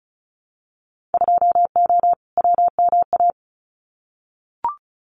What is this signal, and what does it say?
Morse code, heard as a keyed tone, about 34 wpm, 700 Hz. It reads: 2O WMA
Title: Winter Field Day Exchanges Encore 35wpm